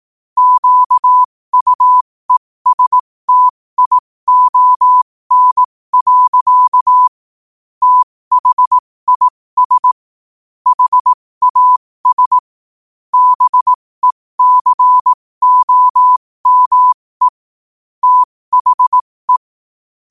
Morse code, heard as a keyed tone, about 18 wpm, spaced as for 15 wpm, 1000 Hz. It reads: QUESTION. THIS HAS BECOME THE
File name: qotdfiles / 15wpm